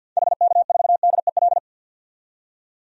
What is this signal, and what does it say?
Morse code, heard as a keyed tone, about 48 wpm, 700 Hz. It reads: HK4DEL
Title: DXCC-100 Call Signs - 6 Characters - 50wpm